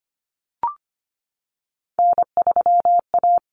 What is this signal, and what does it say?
Morse code, heard as a keyed tone, about 25 wpm, 700 Hz. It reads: N3A
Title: Common Cut Numbers - 3-Digit 25wpm